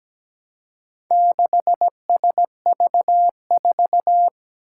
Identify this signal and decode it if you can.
Morse code, heard as a keyed tone, about 17 wpm, 700 Hz. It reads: 6SV4
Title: Four Challenging Letter-Number Combinations 17wpm